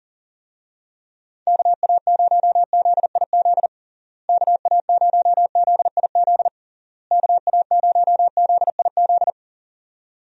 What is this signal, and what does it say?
Morse code, heard as a keyed tone, about 40 wpm, 700 Hz. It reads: KA0ZIZ KA0ZIZ KA0ZIZ
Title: USA Call Signs - 6 Characters - Repeated 3x - 40wpm